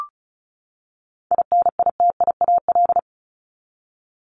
Morse code, heard as a keyed tone, about 34 wpm, 700 Hz. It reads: INITIAL
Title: Top 1501-1750 Words 35wpm